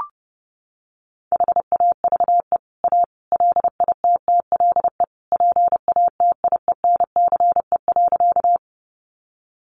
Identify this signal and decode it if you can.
Morse code, heard as a keyed tone, about 30 wpm, 700 Hz. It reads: HAVE A LITTLE PATIENCE.
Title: Sentences from Top 1300 Words 30wpm